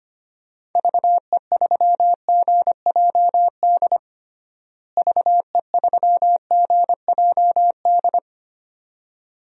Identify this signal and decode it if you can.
Morse code, heard as a keyed tone, about 25 wpm, 700 Hz. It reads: VE3GJD VE3GJD
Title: DXCC-100 Call Signs - 6 Characters - Repeated 2x - 25wpm